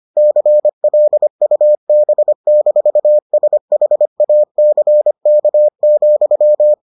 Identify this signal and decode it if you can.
Morse code, heard as a keyed tone, about 25 wpm, 600 Hz. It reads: CLUB-SHACK,